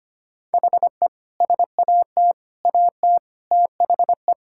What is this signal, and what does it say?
Morse code, heard as a keyed tone, about 25 wpm, 700 Hz. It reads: HE SAT AT THE
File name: practice / 25wpm / Sentences with 3-Word Phrases - 25wpm